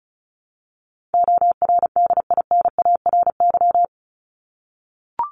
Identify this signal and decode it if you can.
Morse code, heard as a keyed tone, about 34 wpm, 700 Hz. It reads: ORDINARY
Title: Top 1751-2000 Words 35wpm